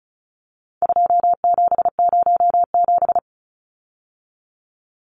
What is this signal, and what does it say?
Morse code, heard as a keyed tone, about 34 wpm, 700 Hz. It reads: 2707
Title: Numbers - 4-Digit 35wpm